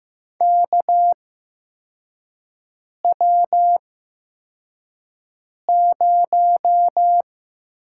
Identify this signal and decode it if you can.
Morse code, heard as a keyed tone, about 15 wpm, spaced as for 4 wpm, 700 Hz. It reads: KW0
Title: USA Call Signs - 4 Characters - 8x Character Spacing - 15wpm